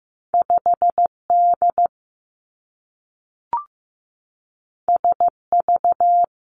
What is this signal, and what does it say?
Morse code, heard as a keyed tone, about 15 wpm, 700 Hz. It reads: 5 D S V
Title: Two Challenging Letter-Number Combinations 15wpm